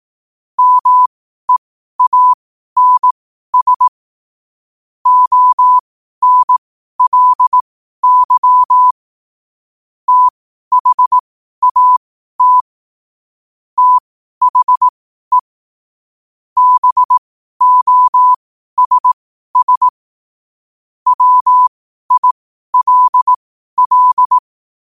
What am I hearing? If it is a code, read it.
Morse code, heard as a keyed tone, about 18 wpm, spaced as for 12 wpm, 1000 Hz. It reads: MEANS ONLY THAT THE BOSS WILL
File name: 12wpm